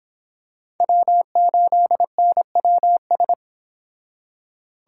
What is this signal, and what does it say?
Morse code, heard as a keyed tone, about 26 wpm, 700 Hz. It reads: W8NWS